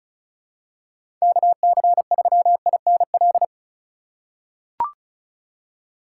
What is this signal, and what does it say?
Morse code, heard as a keyed tone, about 36 wpm, 700 Hz. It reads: KC3INL
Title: USA Call Signs - 2x Character Spacing - 35wpm